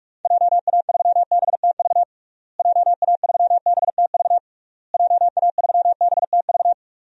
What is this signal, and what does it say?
Morse code, heard as a keyed tone, about 46 wpm, 700 Hz. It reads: JA3BTV JA3BTV JA3BTV
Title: DXCC-100 Call Signs - 6 Characters - Repeated 3x - 45wpm